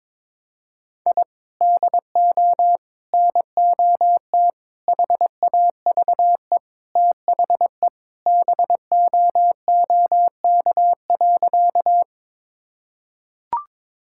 Morse code, heard as a keyed tone, about 22 wpm, 700 Hz. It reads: I DO NOT HAVE THE BOOK.
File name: practice / 22wpm / Sentences with 4-Word Phrases - 22wpm